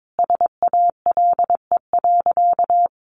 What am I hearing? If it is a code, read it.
Morse code, heard as a keyed tone, about 22 wpm, 700 Hz. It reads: SALE.
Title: Sentences from Top 1400 Words 22wpm